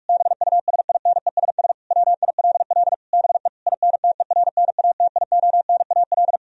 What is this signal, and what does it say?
Morse code, heard as a keyed tone, about 46 wpm, 700 Hz. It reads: BUSINESS WILL BE INTERNATIONAL
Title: Sentences from Top 1000 Words - Review Part 4 - 45wpm